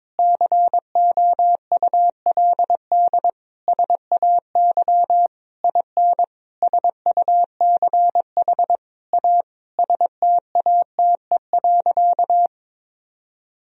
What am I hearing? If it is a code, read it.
Morse code, heard as a keyed tone, about 22 wpm, 700 Hz. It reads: COULD SAY IN SUCH A STATE.
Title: Sentences from Top 1000 Words - Review Part 2 - 22wpm